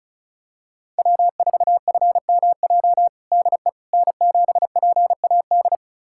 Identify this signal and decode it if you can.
Morse code, heard as a keyed tone, about 34 wpm, 700 Hz. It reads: W4FMJ DE N7PAD
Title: USA Callsign DE Callsign - 5 Characters - 35wpm